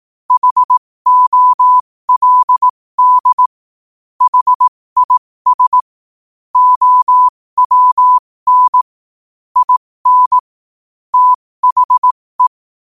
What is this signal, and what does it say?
Morse code, heard as a keyed tone, about 18 wpm, spaced as for 15 wpm, 1000 Hz. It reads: HOLD HIS OWN IN THE
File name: qotdfiles / 15wpm